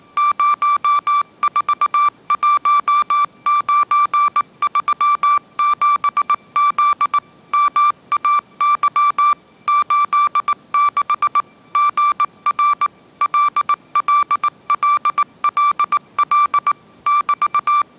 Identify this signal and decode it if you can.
Morse code, heard as a keyed tone, about 18 wpm, 1170 Hz. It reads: 041937Z MAY 86 GR LLLLL <BT>
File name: 20wpm